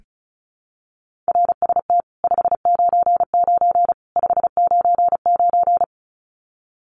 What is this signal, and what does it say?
Morse code, heard as a keyed tone, about 34 wpm, 700 Hz. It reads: RST 599 599
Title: Common QSO Phrases 35wpm